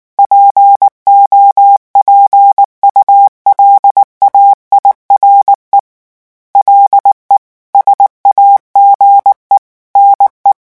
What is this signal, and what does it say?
Morse code, heard as a keyed tone, about 19 wpm, 800 Hz. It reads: POPULAIRE LE SAGE NE
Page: REVISION 8 - 8 mots minute